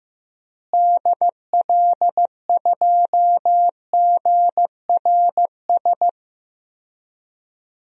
Morse code, heard as a keyed tone, about 15 wpm, 700 Hz. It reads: DL2GRS
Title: DXCC-100 Call Signs - 6 Characters - Repeated 3x - 15wpm